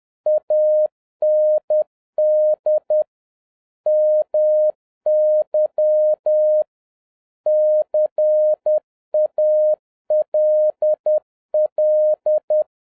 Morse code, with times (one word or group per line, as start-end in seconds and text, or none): 0.3-3.0 AND
3.9-6.6 MY
7.5-12.6 CALL